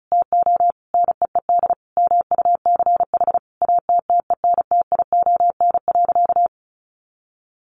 Morse code, heard as a keyed tone, about 34 wpm, 700 Hz. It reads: TO NEED MUCH ATTENTION.
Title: Sentences from Top 1000 Words 35wpm